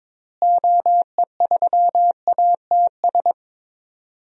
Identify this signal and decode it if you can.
Morse code, heard as a keyed tone, about 23 wpm, 700 Hz. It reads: OE3ATS
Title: DXCC-100 Call Signs - 6 Characters - Repeated 3x - 22wpm